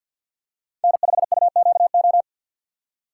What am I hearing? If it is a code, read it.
Morse code, heard as a keyed tone, about 50 wpm, 700 Hz. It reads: N5UQO
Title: USA Call Signs - 5 Characters - 6x Character Spacing - 50wpm